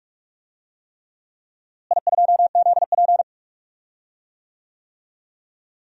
Morse code, heard as a keyed tone, about 44 wpm, 700 Hz. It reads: I2ZP
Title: DXCC-100 Call Signs - 4 Characters - Repeated 3x - 45wpm